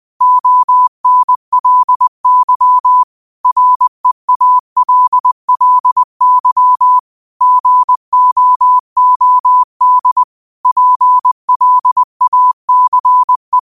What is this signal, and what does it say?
Morse code, heard as a keyed tone, about 20 wpm, 1000 Hz. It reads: ONLY REALLY GOOD PLACE